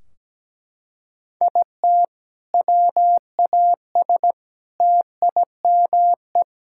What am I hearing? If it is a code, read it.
Morse code, heard as a keyed tone, about 17 wpm, 700 Hz. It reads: IT WAS TIME
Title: Sentences from Top 1600 Words 17wpm